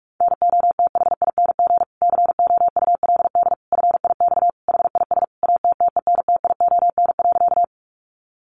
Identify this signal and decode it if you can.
Morse code, heard as a keyed tone, about 44 wpm, 700 Hz. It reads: NOTHING COULD FIX HIS ATTENTION.
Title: Sentences from Top 1400 Words 45wpm